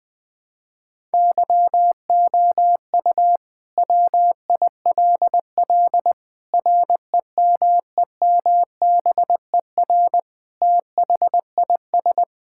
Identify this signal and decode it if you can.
Morse code, heard as a keyed tone, about 19 wpm, 700 Hz. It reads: YOU WILL REMEMBER THIS
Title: Sentences with 5-Word Phrases - 20wpm